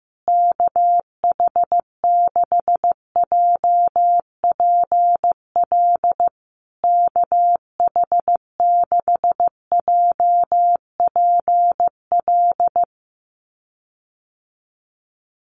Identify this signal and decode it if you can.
Morse code, heard as a keyed tone, about 15 wpm, 700 Hz. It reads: KH6JPL KH6JPL
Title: USA Call Signs Repeated 3x 15wpm